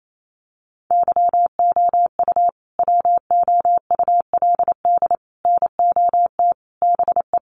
Morse code, heard as a keyed tone, about 28 wpm, 700 Hz. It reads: YOU WOULD NOT BE
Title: Sentences from Top 1200 Words 28wpm